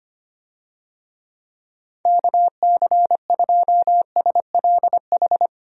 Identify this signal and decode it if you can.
Morse code, heard as a keyed tone, about 25 wpm, 700 Hz. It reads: KC2SLH
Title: USA Call Signs - 6 Characters - 25wpm